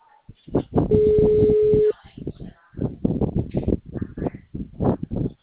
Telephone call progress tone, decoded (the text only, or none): ringback tone